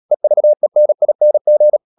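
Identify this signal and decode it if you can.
Morse code, heard as a keyed tone, about 37 wpm, 600 Hz. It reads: EVENING
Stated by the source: Morse practice at 36 wpm, text of Ken Follett's novel Never, from CW med Gnister.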